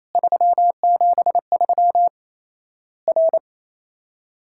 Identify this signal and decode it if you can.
Morse code, heard as a keyed tone, about 27 wpm, 700 Hz. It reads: 373 R
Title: Worked All Europe DX Contest - QTC Exchange - 28wpm